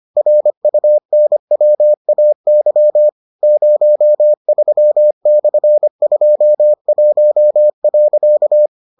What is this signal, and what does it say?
Morse code, heard as a keyed tone, about 25 wpm, 600 Hz. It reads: RUNWAY 03/21.